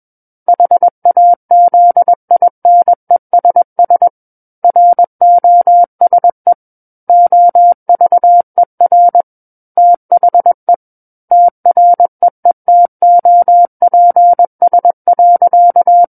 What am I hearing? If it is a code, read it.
Morse code, heard as a keyed tone, about 21 wpm, 700 Hz. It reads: HAZINESS ROSE OVER THE TREETOPS.
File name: Book One / 12-Chapter 12 - 21 WPM